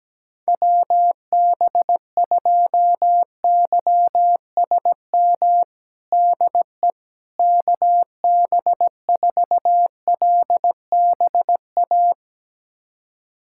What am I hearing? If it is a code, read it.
Morse code, heard as a keyed tone, about 17 wpm, 700 Hz. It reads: WB2YSM DE KB4LBA
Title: USA Callsign DE Callsign 17wpm